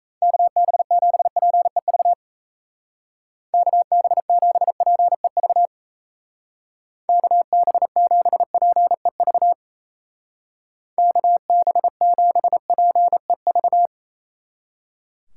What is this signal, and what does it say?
Morse code, heard as a keyed tone, about 35 wpm, 700 Hz. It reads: KB7PEV KB7PEV KB7PEV KB7PEV